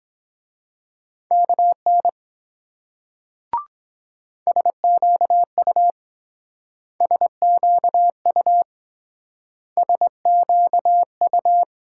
Morse code, heard as a keyed tone, about 22 wpm, 700 Hz. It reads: KN SQU SQU SQU